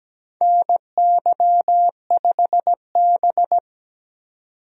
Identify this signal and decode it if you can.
Morse code, heard as a keyed tone, about 17 wpm, 700 Hz. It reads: NY5B
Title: USA Call Signs - 4 Characters - 17wpm